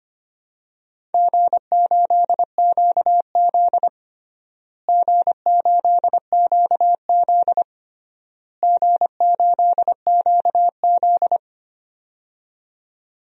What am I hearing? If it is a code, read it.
Morse code, heard as a keyed tone, about 25 wpm, 700 Hz. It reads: G8QZ G8QZ G8QZ
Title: DXCC-100 Call Signs - 4 Characters - Repeated 3x - 25wpm